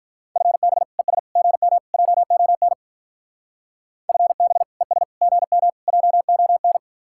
Morse code, heard as a keyed tone, about 44 wpm, 700 Hz. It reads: FB ES GM JON FB ES GM JON
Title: Speed-Racing Common QSO Phrases Encore - 35wpm